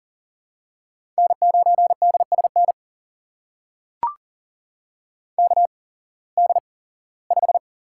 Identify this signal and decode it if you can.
Morse code, heard as a keyed tone, about 40 wpm, spaced as for 21 wpm, 700 Hz. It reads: N9DSN KD5
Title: USA Call Signs - 5 Characters - 8x Character Spacing - 40wpm